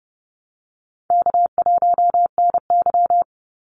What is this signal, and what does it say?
Morse code, heard as a keyed tone, about 30 wpm, 700 Hz. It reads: K1NY
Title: Easy Call Signs Repeated 3x - 30wpm